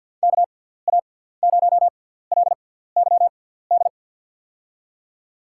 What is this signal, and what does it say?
Morse code, heard as a keyed tone, about 53 wpm, spaced as for 17 wpm, 700 Hz. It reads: KA0LYD
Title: USA Call Signs - 6x Character Spacing - 50wpm